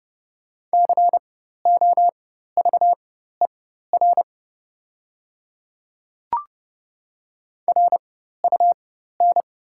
Morse code, heard as a keyed tone, about 30 wpm, spaced as for 14 wpm, 700 Hz. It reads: COVER RUN
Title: 4x Character Spacing - Top 300 Words - 30wpm